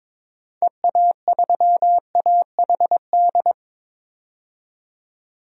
Morse code, heard as a keyed tone, about 22 wpm, 700 Hz. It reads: EA3AHD
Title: DXCC-100 Call Signs - 6 Characters - 22wpm